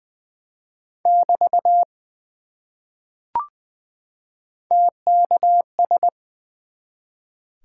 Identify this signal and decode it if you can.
Morse code, heard as a keyed tone, about 20 wpm, 700 Hz. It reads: <BT> TKS
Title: Top 90 QSO Elements 20wpm